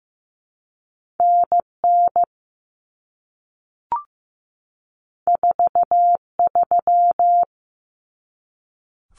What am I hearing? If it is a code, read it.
Morse code, heard as a keyed tone, about 15 wpm, 700 Hz. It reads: NN 43
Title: Common Cut Numbers - 2-Digit 15wpm